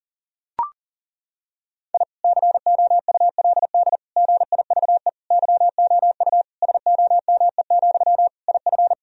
Morse code, heard as a keyed tone, about 37 wpm, 700 Hz. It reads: I COULD GIVE YOU SOME, IF
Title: Sentences from Top 1000 Words - Review Part 1 - 40wpm